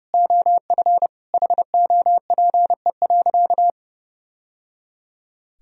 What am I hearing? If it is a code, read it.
Morse code, heard as a keyed tone, about 28 wpm, 700 Hz. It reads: OF HOPE.
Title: Sentences from Top 1000 Words - Review Part 3 - 30wpm